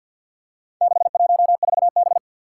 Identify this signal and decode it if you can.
Morse code, heard as a keyed tone, about 50 wpm, 700 Hz. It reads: 614B
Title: Four Challenging Letter-Number Combinations 50wpm